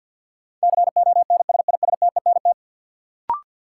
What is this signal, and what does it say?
Morse code, heard as a keyed tone, about 53 wpm, 700 Hz. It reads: CONSISTENT
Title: Sentences from Top 1500 Words - New Word Drill Alt - 50wpm